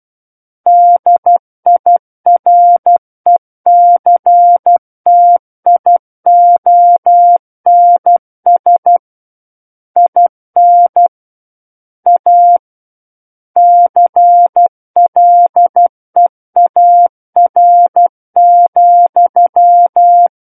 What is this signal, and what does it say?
Morse code, heard as a keyed tone, about 12 wpm, 700 Hz. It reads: DIRECTIONS IN A CLEAR,
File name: Book One / 03-Chapter 3 - 12 WPM